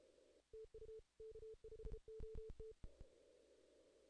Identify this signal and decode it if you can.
Morse code, heard as a keyed tone, about 32 wpm, 440 Hz. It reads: TU K5OT
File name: 2016 JAN CW NAQP / 2016 Jan naqp k5ot - 0-11